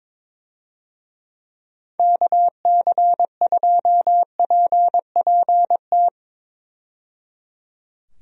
Morse code, heard as a keyed tone, about 22 wpm, 700 Hz. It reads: KC2PPT
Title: USA Call Signs - 6 Characters - 22wpm